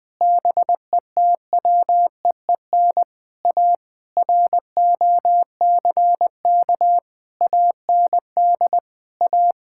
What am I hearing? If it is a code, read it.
Morse code, heard as a keyed tone, about 20 wpm, 700 Hz. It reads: BETWEEN A ROCK AND A
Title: Sets of 8 Words - 20wpm